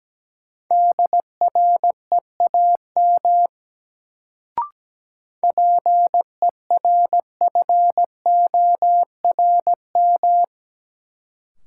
Morse code, heard as a keyed tone, about 17 wpm, 700 Hz. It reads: DREAM PERFORM
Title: Sentences from Top 1500 Words - New Word Drill Alt - 17wpm